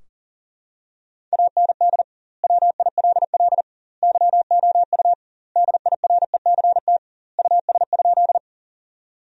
Morse code, heard as a keyed tone, about 40 wpm, 700 Hz. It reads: AND WILL YOU DIRECT US?
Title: 2x Word-Spacing - Sentences from Top 400 Words - 40wpm